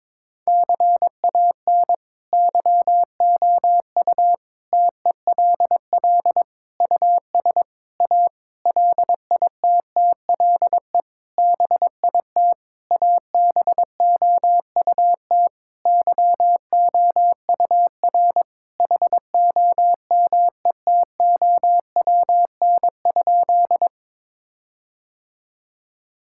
Morse code, heard as a keyed tone, about 21 wpm, 700 Hz. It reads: CAN YOU TELL US A LITTLE BIT ABOUT YOUR HOMETOWN?
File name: practice / 22wpm / Sentences with 8-Word Phrases - 22wpm